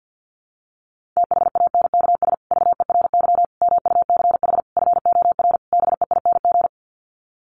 Morse code, heard as a keyed tone, about 50 wpm, 700 Hz. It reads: THANKS VERY MUCH FOR BEING
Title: Sets of 5 Words Encore 50wpm